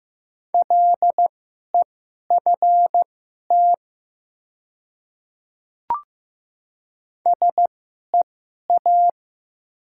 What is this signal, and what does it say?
Morse code, heard as a keyed tone, about 15 wpm, spaced as for 11 wpm, 700 Hz. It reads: LEFT SEA
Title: 2x Character Spacing - Top 300 Words - 15wpm